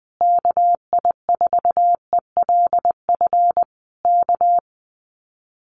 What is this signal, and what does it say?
Morse code, heard as a keyed tone, about 20 wpm, 700 Hz. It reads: KI4ELF K
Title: Common QSO Phrases Encore 20wpm